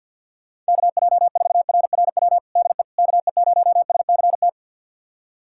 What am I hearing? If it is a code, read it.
Morse code, heard as a keyed tone, about 51 wpm, 700 Hz. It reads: KJ4RRW DE KE0SCT
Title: USA Callsign DE Callsign - 6 Characters - 50wpm